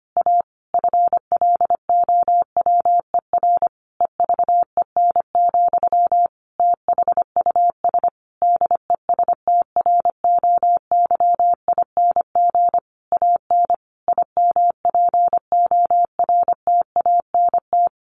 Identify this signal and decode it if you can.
Morse code, heard as a keyed tone, about 25 wpm, 700 Hz. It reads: A FLOWER EVEN, THUS DESTROYING AN IMPORTANT
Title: A Sound Of Thunder 25wpm